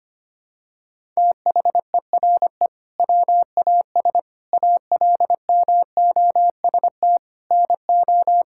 Morse code, heard as a keyed tone, about 25 wpm, 700 Hz. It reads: THERE WAS ALMOST NO
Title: Sentences from Top 1000 Words - Review Part 6 - 25wpm